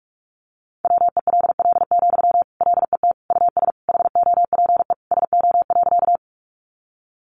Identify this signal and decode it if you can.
Morse code, heard as a keyed tone, about 44 wpm, 700 Hz. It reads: WELL, LET US HOPE SO.
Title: Sentences from Top 1000 Words 45wpm